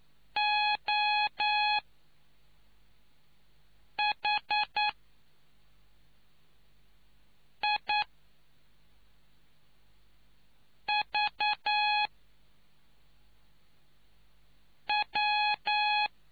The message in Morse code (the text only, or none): O H I V W